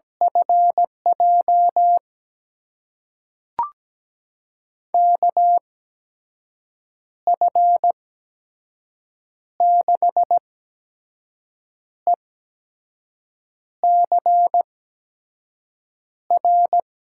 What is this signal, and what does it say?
Morse code, heard as a keyed tone, about 17 wpm, spaced as for 5 wpm, 700 Hz. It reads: FJ KF6ECR
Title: USA Call Signs - 6 Characters - 8x Character Spacing - 17wpm